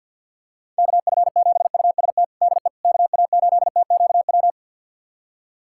Morse code, heard as a keyed tone, about 51 wpm, 700 Hz. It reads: KF7UST DE KA8TQW
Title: USA Callsign DE Callsign 50wpm